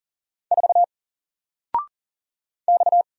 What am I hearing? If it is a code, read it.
Morse code, heard as a keyed tone, about 41 wpm, 700 Hz. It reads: 4 X
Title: Single Letter-Number Alt 40wpm